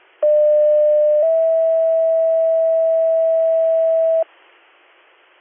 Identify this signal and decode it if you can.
Two-tone page, heard as a one-second tone, then a three-second tone, about 600 then 650 Hz. It is heard from 0.2 to 4.2 s.